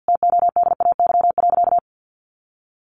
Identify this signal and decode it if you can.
Morse code, heard as a keyed tone, about 48 wpm, 700 Hz. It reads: TODAY.
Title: Sentences from Top 1400 Words 50wpm